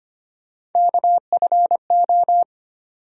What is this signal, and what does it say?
Morse code, heard as a keyed tone, about 25 wpm, 700 Hz. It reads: KFO
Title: USA Call Sign Postfixes - 3 Letters 25wpm